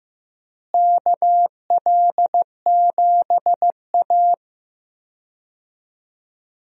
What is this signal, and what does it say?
Morse code, heard as a keyed tone, about 15 wpm, 700 Hz. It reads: KL7A
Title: DXCC-100 Call Signs - 4 Characters - Repeated 3x - 15wpm